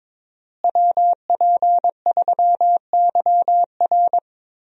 Morse code, heard as a keyed tone, about 22 wpm, 700 Hz. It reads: WP3YR
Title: USA Call Signs - 5 Characters - Repeated 2x - 22wpm